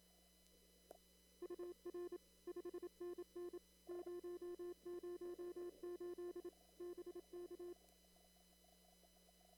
Morse code, heard as a keyed tone, about 27 wpm, 350 Hz. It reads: UR 5NN 008 BK
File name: CW / kt5j wpx cw-op-k5pi